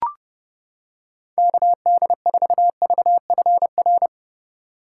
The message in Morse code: KD4VFR

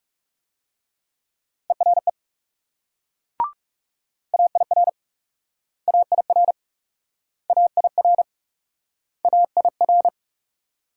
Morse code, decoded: ERE AIR AIR AIR AIR

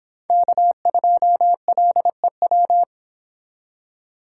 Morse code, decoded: K2LEW

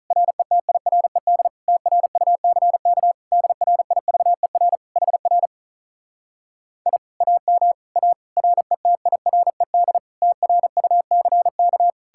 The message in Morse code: RETIRED TRUCK DRIVER HR I AM A RETIRED TRUCK

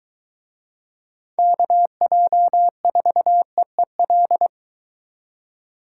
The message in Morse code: KJ4EEL

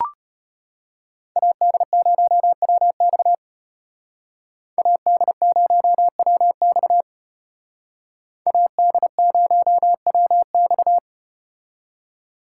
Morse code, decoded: AD0WX AD0WX AD0WX

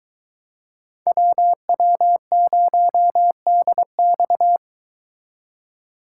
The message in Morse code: WW0DX

